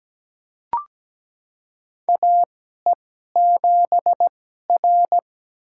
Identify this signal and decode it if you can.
Morse code, heard as a keyed tone, about 17 wpm, spaced as for 12 wpm, 700 Hz. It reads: AE7R
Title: USA Call Signs - 4 Characters - 2x Character Spacing - 17wpm